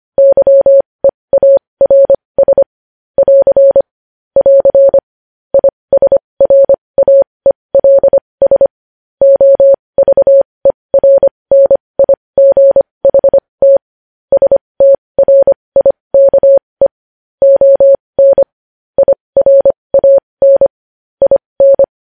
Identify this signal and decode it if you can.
Morse code, heard as a keyed tone, about 25 wpm, spaced as for 20 wpm, 550 Hz. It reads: YEARS <AR> <AR> ISRAELS OVERNIGHT STRIKE ON IRAN IN
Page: News Headlines in Morse Code at 20 WPM